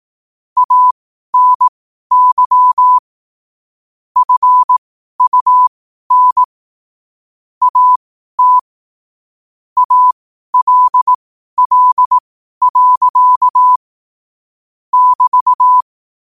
Morse code, heard as a keyed tone, about 18 wpm, spaced as for 12 wpm, 1000 Hz. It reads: ANY FUN AT ALL. <BT>
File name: qotdfiles / 12wpm